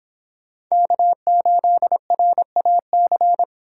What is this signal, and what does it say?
Morse code, heard as a keyed tone, about 26 wpm, 700 Hz. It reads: K8RAC